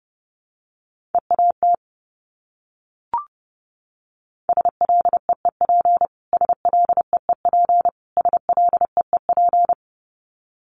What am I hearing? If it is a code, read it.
Morse code, heard as a keyed tone, about 30 wpm, 700 Hz. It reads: EAT SLEEP SLEEP SLEEP
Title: Top 500 Words Repeated 30wpm